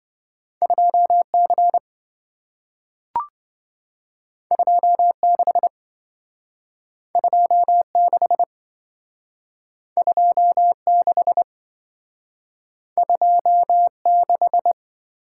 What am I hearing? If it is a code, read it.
Morse code, heard as a keyed tone, about 27 wpm, 700 Hz. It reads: 2C 26 26 26 26